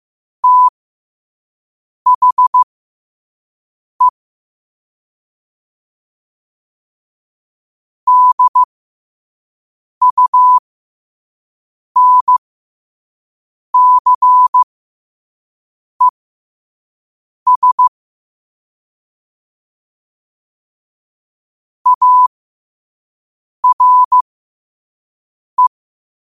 Morse code, heard as a keyed tone, about 15 wpm, spaced as for 5 wpm, 1000 Hz. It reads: THE DUNCES ARE
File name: qotdfiles / 5wpm